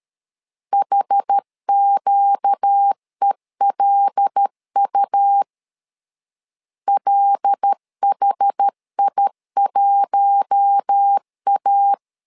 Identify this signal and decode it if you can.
Morse code, heard as a keyed tone, about 13 wpm, 800 Hz. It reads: HQELU LHI1A
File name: LEZIONI / 5 LBVC / 5 LB 12wpm 1906